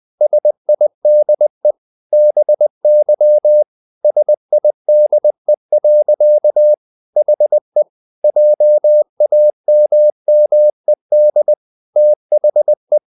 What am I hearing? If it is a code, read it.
Morse code, heard as a keyed tone, about 20 wpm, 600 Hz. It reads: SIDE BY SIDE. HE JAMMED THE
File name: the old man and the sea part 2 20wpm